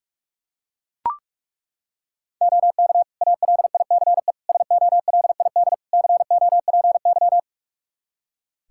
Morse code, heard as a keyed tone, about 41 wpm, 700 Hz. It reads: OK ALICE SOLID COPY
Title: Practice QSO - Exchange 3 - More Info - 45wpm